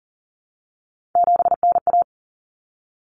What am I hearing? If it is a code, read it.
Morse code, heard as a keyed tone, about 40 wpm, 700 Hz. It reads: 7NA
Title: Common Cut Numbers - 3-Digit 40wpm